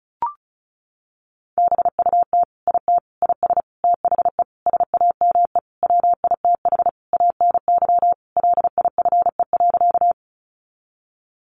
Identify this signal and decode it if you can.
Morse code, heard as a keyed tone, about 34 wpm, 700 Hz. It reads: BUT IT IS THE SAME WITH ANY LIFE.
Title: Sentences from Top 300 Words 35wpm